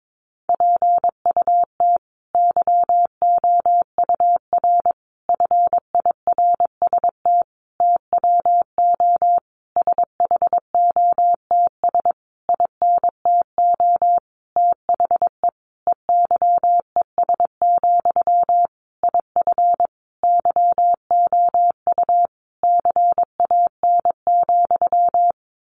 Morse code, heard as a keyed tone, about 22 wpm, 700 Hz. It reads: PUT YOUR FIRST TWO SHOTS INTO THE EYES, IF YOU CAN,
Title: A Sound Of Thunder 22wpm